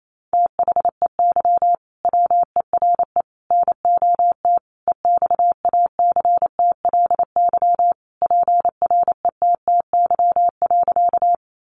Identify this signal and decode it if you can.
Morse code, heard as a keyed tone, about 28 wpm, 700 Hz. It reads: THEY WERE NOT EXACTLY PRETTY.
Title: Sentences from Top 1000 Words 28wpm